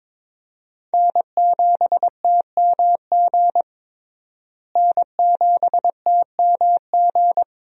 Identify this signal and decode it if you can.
Morse code, heard as a keyed tone, about 22 wpm, 700 Hz. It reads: N7TMG N7TMG
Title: USA Call Signs - 5 Characters - Repeated 3x - 22wpm